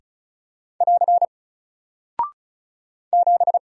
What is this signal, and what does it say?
Morse code, heard as a keyed tone, about 36 wpm, 700 Hz. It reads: <AR> 7
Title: Single Character (Letter:Number:Punctuation:Prosign) Alt 35wpm